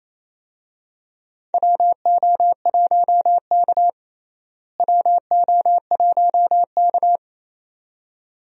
Morse code, heard as a keyed tone, about 28 wpm, 700 Hz. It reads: WO1K WO1K
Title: USA Call Signs - 4 Characters - Repeated 2x - 28wpm